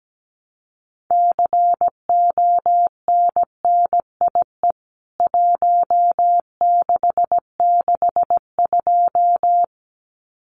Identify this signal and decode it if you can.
Morse code, heard as a keyed tone, about 17 wpm, 700 Hz. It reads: CONNIE 1662
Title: CWT Exchange 17wpm